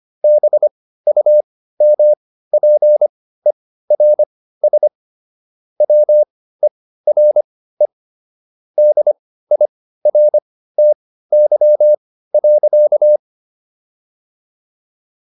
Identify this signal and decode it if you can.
Morse code, heard as a keyed tone, about 25 wpm, spaced as for 15 wpm, 600 Hz. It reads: BUMPERS WERE DIRTY.